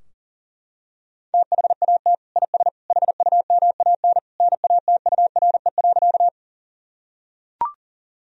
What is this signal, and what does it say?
Morse code, heard as a keyed tone, about 37 wpm, 700 Hz. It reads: THAT IS HUMAN NATURE.
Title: Sentences from Top 1000 Words - Review Part 3 - 40wpm